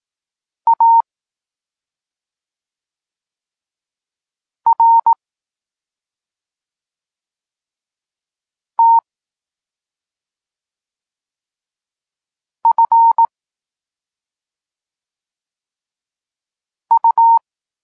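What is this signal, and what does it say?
Morse code, heard as a keyed tone, about 18 wpm, 920 Hz. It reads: A R T F U
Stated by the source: Polish Morse lesson teaching the letter C, from SP9MOA Niepołomicki Klub Krótkofalowców.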